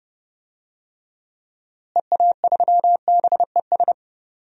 Morse code, heard as a keyed tone, about 31 wpm, 700 Hz. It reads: EA3BES
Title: DXCC-100 Call Signs - 6 Characters - Repeated 3x - 30wpm